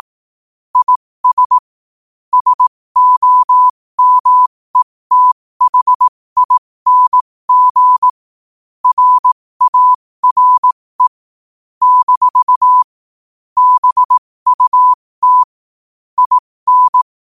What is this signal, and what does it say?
Morse code, heard as a keyed tone, about 18 wpm, spaced as for 15 wpm, 1000 Hz. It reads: IS SOMETHING RARE - BUT IN
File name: qotdfiles / 15wpm